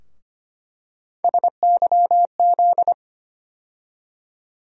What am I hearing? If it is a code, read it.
Morse code, heard as a keyed tone, about 25 wpm, 700 Hz. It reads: SYZ
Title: USA Call Sign Postfixes - 3 Letters 25wpm